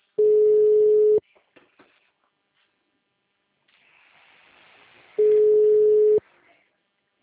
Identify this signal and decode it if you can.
Ringback tone (Europe, 425 Hz).
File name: out- -2008- - - .2410